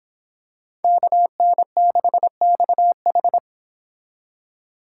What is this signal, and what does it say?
Morse code, heard as a keyed tone, about 26 wpm, 700 Hz. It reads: KN6XH